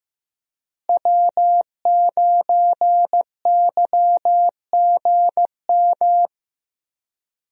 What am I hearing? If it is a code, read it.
Morse code, heard as a keyed tone, about 16 wpm, 700 Hz. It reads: W9YGM
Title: USA Call Signs - 5 Characters - Repeated 3x - 15wpm